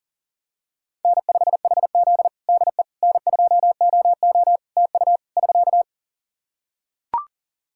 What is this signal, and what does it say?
Morse code, heard as a keyed tone, about 37 wpm, 700 Hz. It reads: N5HZ DE N2OO TU <SK>
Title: Practice QSO - Exchange 4 - Ending - 40wpm